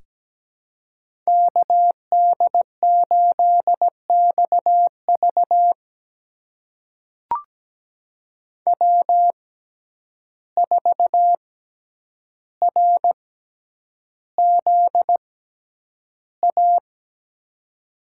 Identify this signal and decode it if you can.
Morse code, heard as a keyed tone, about 17 wpm, spaced as for 9 wpm, 700 Hz. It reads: KD8XV W4RZA
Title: USA Call Signs - 5 Characters - 6x Character Spacing - 17wpm